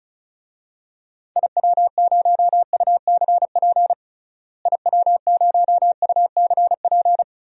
Morse code, heard as a keyed tone, about 36 wpm, 700 Hz. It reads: IW0UCP IW0UCP
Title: DXCC-100 Call Signs - 6 Characters - Repeated 3x - 35wpm